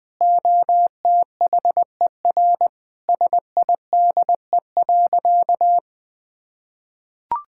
Morse code, heard as a keyed tone, about 21 wpm, 700 Hz. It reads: OTHER SIDE.
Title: Sentences with 2-Word Phrases - 20wpm